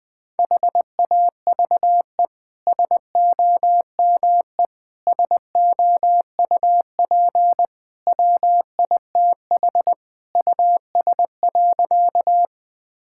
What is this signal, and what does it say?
Morse code, heard as a keyed tone, about 21 wpm, 700 Hz. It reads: HAVE SOME SOUP WITH US.
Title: Sentences from Top 1000 Words - Review Part 6 - 20wpm